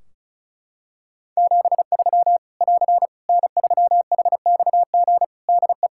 Morse code, heard as a keyed tone, about 34 wpm, 700 Hz. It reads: 73 <AR> N3HXG DE
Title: Practice QSO - Exchange 4 - Ending - 35wpm